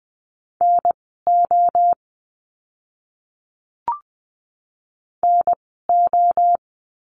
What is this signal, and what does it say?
Morse code, heard as a keyed tone, about 20 wpm, spaced as for 14 wpm, 700 Hz. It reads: NO NO
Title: 024 Lesson 5f - Letter o - Words 20wpm